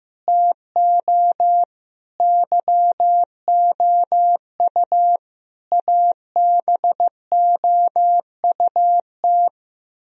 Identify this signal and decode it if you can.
Morse code, heard as a keyed tone, about 16 wpm, 700 Hz. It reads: TO YOU ABOUT
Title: Sentences from Top 1000 Words - Review Part 1 - 15wpm